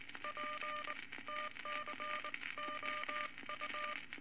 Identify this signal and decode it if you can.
Morse code, heard as a keyed tone, about 19 wpm, spaced as for 17 wpm, 1200 Hz, with noise. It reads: PTCOU
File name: Morse Code Records / 19 B LCM 167 Twelve Words Per Minute